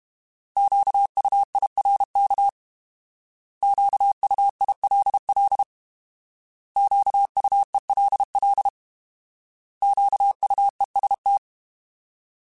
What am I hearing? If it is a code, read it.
Morse code, heard as a keyed tone, about 32 wpm, 800 Hz. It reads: QUIRK QUILL QUELL QUEST
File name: Alphabetical / Alphabetical mp3s / Qs @30